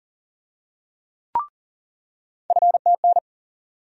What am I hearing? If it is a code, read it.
Morse code, heard as a keyed tone, about 40 wpm, 700 Hz. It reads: FTN
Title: USA Call Sign Postfixes - 3 Letters 40wpm